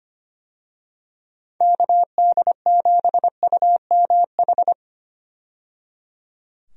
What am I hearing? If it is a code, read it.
Morse code, heard as a keyed tone, about 25 wpm, 700 Hz. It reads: KD7UMH
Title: USA Call Signs - 6 Characters - 25wpm